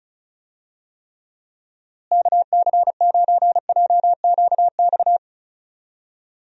Morse code, heard as a keyed tone, about 36 wpm, 700 Hz. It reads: KC9JQX
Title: USA Call Signs - 6 Characters - Repeated 3x - 35wpm